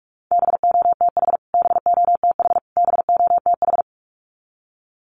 Morse code, heard as a keyed tone, about 44 wpm, 700 Hz. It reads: BOTH BOTH BOTH
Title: Top 500 Words Repeated 45wpm